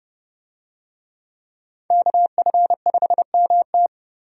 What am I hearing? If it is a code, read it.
Morse code, heard as a keyed tone, about 31 wpm, 700 Hz. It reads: KF5MT
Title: USA Call Signs - 5 Characters - 30wpm